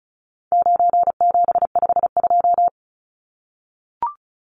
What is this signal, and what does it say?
Morse code, heard as a keyed tone, about 34 wpm, 700 Hz. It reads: 9752
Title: Numbers - 4-Digit 35wpm